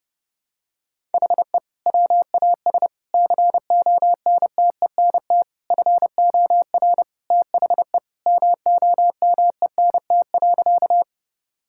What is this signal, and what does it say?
Morse code, heard as a keyed tone, about 30 wpm, 700 Hz. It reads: HE WAS CONTENT FOR THE MOMENT.
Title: Sentences from Top 1000 Words - Review Part 6 - 30wpm